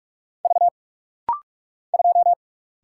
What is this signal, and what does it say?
Morse code, heard as a keyed tone, about 48 wpm, 700 Hz. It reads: V 2
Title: Single Letter-Number - Rapid-Fire 45wpm